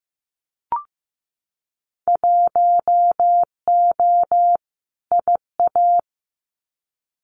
Morse code, heard as a keyed tone, about 15 wpm, 700 Hz. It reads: 1O IA
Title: Winter Field Day Exchanges Encore 15wpm